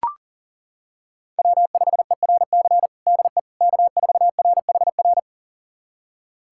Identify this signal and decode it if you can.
Morse code, heard as a keyed tone, about 40 wpm, 700 Hz. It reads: W5ERC DE K4RHR
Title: USA Callsign DE Callsign 40wpm